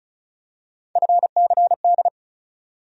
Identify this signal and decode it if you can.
Morse code, heard as a keyed tone, about 34 wpm, 700 Hz. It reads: FCD
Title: USA Call Sign Postfixes - 3 Letters 35wpm